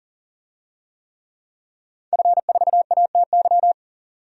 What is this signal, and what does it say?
Morse code, heard as a keyed tone, about 40 wpm, 700 Hz. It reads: F4ATY
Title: DXCC-100 Call Signs - 5 Characters - 40wpm